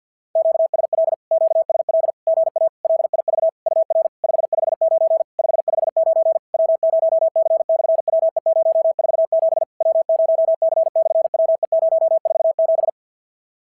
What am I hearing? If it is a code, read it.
Morse code, heard as a keyed tone, about 50 wpm, 650 Hz. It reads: QSL QSL GA LIV UR 559 559 W0C/WE047 W0C/WE047
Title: SOTA QSO Practice 50wpm